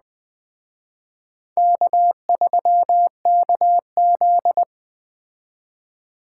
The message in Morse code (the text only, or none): K3KZ